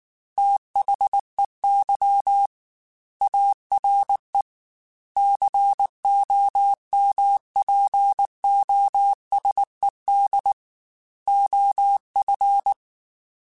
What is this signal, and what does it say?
Morse code, heard as a keyed tone, about 19 wpm, 800 Hz. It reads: THEY ARE COMPOSED OF
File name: Technical Subjects / Technical Subjects-mp3s / Sw Pwr Sup @18